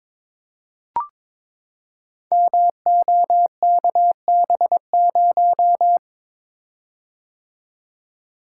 Morse code, heard as a keyed tone, about 22 wpm, 700 Hz. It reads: MOKB0
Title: Code Groups - 5 Letter:Numbers - 22wpm